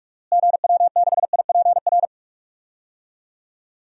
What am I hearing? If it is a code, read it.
Morse code, heard as a keyed tone, about 42 wpm, 700 Hz. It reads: GW6IPR
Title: DXCC-100 Call Signs - 6 Characters - Repeated 3x - 45wpm